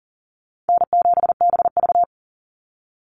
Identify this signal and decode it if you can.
Morse code, heard as a keyed tone, about 40 wpm, 700 Hz. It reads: N7BV
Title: Call Signs - 40wpm